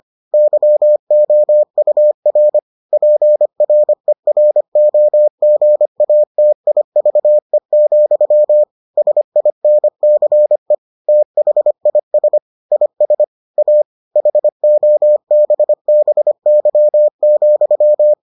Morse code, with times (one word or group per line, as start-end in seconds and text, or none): 0.3-2.6 YOUR
2.9-8.6 PREROGATIVE,
9.0-10.7 SINCE
11.1-12.4 THIS
12.7-13.2 IS
13.6-13.8 A
14.2-18.1 HOBBY,